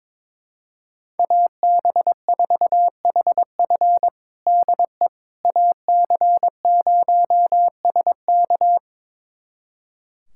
Morse code, heard as a keyed tone, about 21 wpm, 700 Hz. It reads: AB4HF DE AC0SK
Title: USA Callsign DE Callsign - 5 Characters - 22wpm